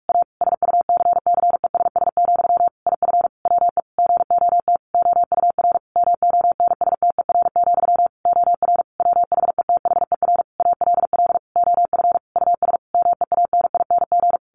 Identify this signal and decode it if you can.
Morse code, heard as a keyed tone, about 44 wpm, 700 Hz. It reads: A SUCCESS, IF WE GOT OUR MONSTER, OR WHETHER ALL OF US MEANING
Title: A Sound Of Thunder 45wpm